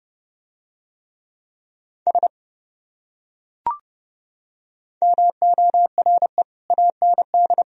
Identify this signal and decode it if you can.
Morse code, heard as a keyed tone, about 29 wpm, 700 Hz. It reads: S MORE AND